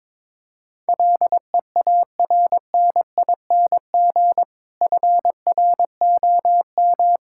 Morse code, heard as a keyed tone, about 22 wpm, 700 Hz. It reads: LEARNING FROM
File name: Difficult Sentences with Hints - 22wpm